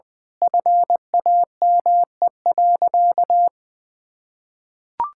Morse code, heard as a keyed tone, about 20 wpm, 700 Hz. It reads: FAME.
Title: Sentences with 6-Word Phrases - 20wpm